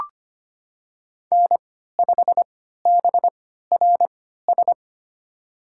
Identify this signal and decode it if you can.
Morse code, heard as a keyed tone, about 25 wpm, spaced as for 14 wpm, 700 Hz. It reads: N5BRS
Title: USA Call Signs - 3x Character Spacing - 25wpm